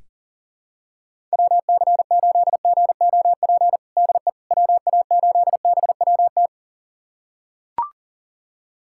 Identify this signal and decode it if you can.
Morse code, heard as a keyed tone, about 40 wpm, 700 Hz. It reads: WC8GOP DE WA8BWT
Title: USA Callsign DE Callsign 40wpm